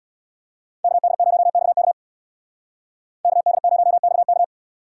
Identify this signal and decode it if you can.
Morse code, heard as a keyed tone, about 64 wpm, 700 Hz. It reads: KD0XK KD0XK